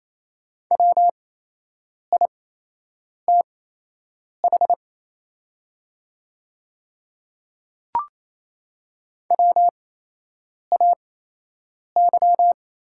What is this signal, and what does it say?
Morse code, heard as a keyed tone, about 27 wpm, spaced as for 8 wpm, 690 Hz. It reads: WITH WAY
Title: 8x Character Spacing - Top 200 Words - 28wpm